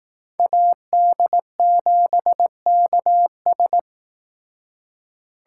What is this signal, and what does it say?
Morse code, heard as a keyed tone, about 18 wpm, 700 Hz. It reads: AD7KS
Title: Speed Racing - USA 2x2 Call Signs 15wpm